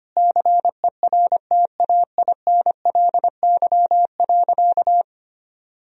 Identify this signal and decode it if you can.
Morse code, heard as a keyed tone, about 25 wpm, 700 Hz. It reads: CERTAINLY.
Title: Sentences from Top 1000 Words - Review Part 1 - 25wpm